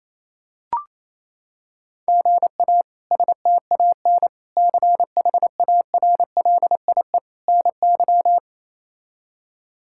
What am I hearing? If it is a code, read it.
Morse code, heard as a keyed tone, about 28 wpm, 700 Hz. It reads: GA STAN CHARLIE NY
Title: K1USN SST QSO Practice - 28wpm